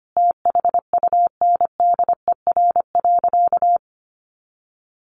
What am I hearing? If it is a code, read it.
Morse code, heard as a keyed tone, about 25 wpm, 700 Hz. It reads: THUNDER.
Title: A Sound Of Thunder 25wpm